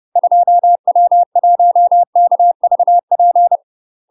Morse code, heard as a keyed tone, about 30 wpm, 700 Hz. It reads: 2W1KVP
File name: UK MASTER QSOs TRAINING / 30WPM / UK MASTER QSOs 0036